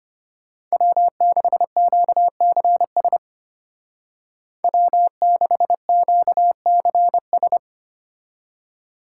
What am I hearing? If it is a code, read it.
Morse code, heard as a keyed tone, about 32 wpm, 700 Hz. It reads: W6QCS W6QCS